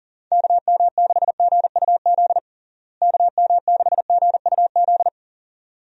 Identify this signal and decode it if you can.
Morse code, heard as a keyed tone, about 40 wpm, 700 Hz. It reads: KM6GUZ KM6GUZ
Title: USA Call Signs - 6 Characters - Repeated 2x - 40wpm